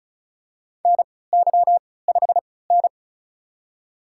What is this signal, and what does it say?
Morse code, heard as a keyed tone, about 34 wpm, spaced as for 20 wpm, 700 Hz. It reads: NY5N
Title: USA Call Signs - 3x Character Spacing - 35wpm